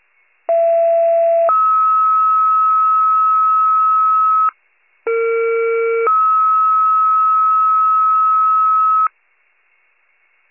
Two-tone page, heard as a one-second tone, then a three-second tone, about 670 then 1290 Hz, about 460 then 1290 Hz. It is heard from 0.5 to 9.1 s.